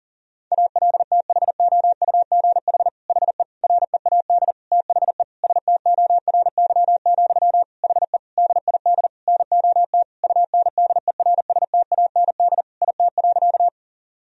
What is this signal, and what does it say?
Morse code, heard as a keyed tone, about 40 wpm, 700 Hz. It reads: ALTHOUGH HE READ THE STORY, HE DID NOT UNDERSTAND IT.
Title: Sentences from Top 1000 Words - Review Part 3 - 40wpm